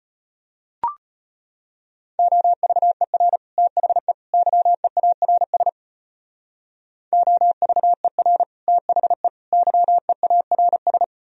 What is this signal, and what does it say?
Morse code, heard as a keyed tone, about 31 wpm, 700 Hz. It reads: OVER THE YEARS OVER THE YEARS
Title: Speed-Racing - Sets of 3 Words 25wpm